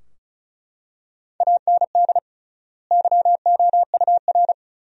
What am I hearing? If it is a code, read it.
Morse code, heard as a keyed tone, about 36 wpm, 700 Hz. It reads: AND YOUR
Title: 3x Word-Spacing - Sentences from Top 300 Words - 35wpm